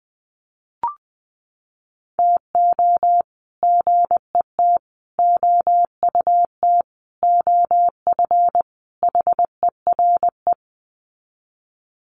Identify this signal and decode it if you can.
Morse code, heard as a keyed tone, about 19 wpm, 700 Hz. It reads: TO GET OUT OF HERE
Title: Sets of 5 Words Encore 20wpm